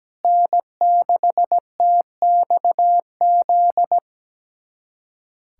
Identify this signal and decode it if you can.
Morse code, heard as a keyed tone, about 17 wpm, 700 Hz. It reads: N6TXZ
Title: USA Callsign DE Callsign - 5 Characters - 17wpm